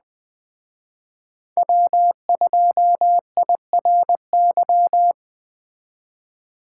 Morse code, heard as a keyed tone, about 20 wpm, 700 Hz. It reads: W2IRY
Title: Speed Racing - USA 1x3 Call Signs 17wpm